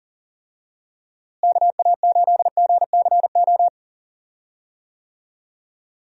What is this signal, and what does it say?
Morse code, heard as a keyed tone, about 40 wpm, 700 Hz. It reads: KA8GCO
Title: USA Call Signs - 6 Characters - Repeated 2x - 40wpm